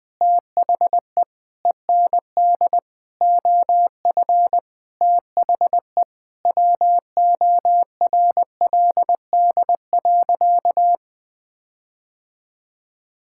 Morse code, heard as a keyed tone, about 20 wpm, 700 Hz. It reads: THE END OF THE WORLD.
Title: Sentences with 5-Word Phrases - 20wpm